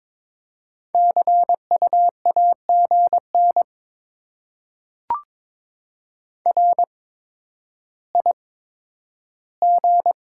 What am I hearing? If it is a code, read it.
Morse code, heard as a keyed tone, about 22 wpm, spaced as for 11 wpm, 700 Hz. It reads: CU AGN RIG
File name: Top 20 QSO Elements - 8x Character Spacing - 22wpm